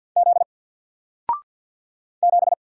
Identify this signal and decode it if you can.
Morse code, heard as a keyed tone, about 47 wpm, 700 Hz. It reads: Z 7
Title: Single Letter-Number 50wpm